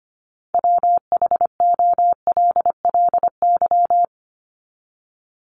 Morse code, heard as a keyed tone, about 25 wpm, 700 Hz. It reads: WHOLLY
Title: LY Suffix - Top 100 Words with Base Words 25wpm